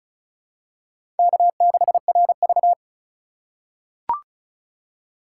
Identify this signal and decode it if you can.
Morse code, heard as a keyed tone, about 36 wpm, 700 Hz. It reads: K6RV
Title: USA Call Signs - 4 Characters - 2x Character Spacing - 35wpm